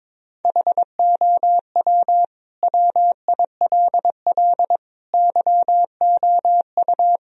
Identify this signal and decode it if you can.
Morse code, heard as a keyed tone, about 22 wpm, 700 Hz. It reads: HOW WILL YOU
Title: Sentences from Top 1000 Words - Review Part 2 - 22wpm